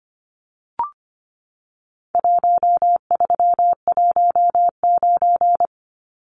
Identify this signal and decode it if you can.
Morse code, heard as a keyed tone, about 25 wpm, 700 Hz. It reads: 1319
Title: Numbers - Contest Style - 4-Digit 25wpm